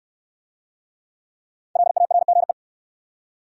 Morse code, heard as a keyed tone, about 63 wpm, 700 Hz. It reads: LARGE